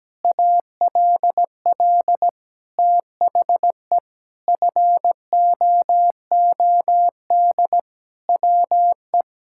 Morse code, heard as a keyed tone, about 17 wpm, 700 Hz. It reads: ALL THE FOOD WE
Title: Sentences from Top 1000 Words - Review Part 5 - 17wpm